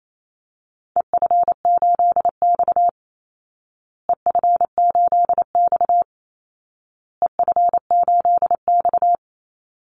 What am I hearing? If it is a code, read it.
Morse code, heard as a keyed tone, about 28 wpm, 700 Hz. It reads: EF8X EF8X EF8X
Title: Call Signs Repeated 2x - 28wpm